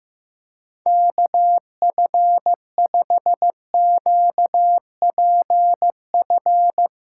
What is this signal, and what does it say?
Morse code, heard as a keyed tone, about 15 wpm, 700 Hz. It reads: KF5QPF
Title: USA Callsign DE Callsign 15wpm